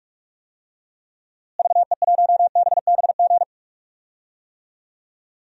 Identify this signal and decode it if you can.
Morse code, heard as a keyed tone, about 46 wpm, 700 Hz. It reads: VE1BBG
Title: DXCC-100 Call Signs - 6 Characters - Repeated 3x - 45wpm